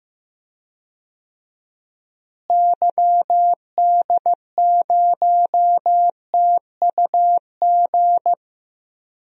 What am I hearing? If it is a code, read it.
Morse code, heard as a keyed tone, about 15 wpm, 700 Hz. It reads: YD0TUG
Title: DXCC-100 Call Signs - 6 Characters - Repeated 2x - 15wpm